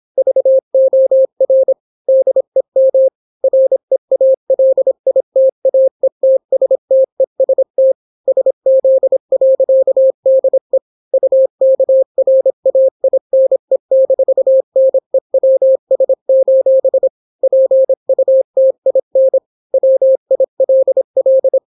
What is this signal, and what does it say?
Morse code, heard as a keyed tone, about 26 wpm, 530 Hz. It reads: VOR DEM REALITAETSTEST SZ.DE UKRAINE-NEWS: PUTIN WILL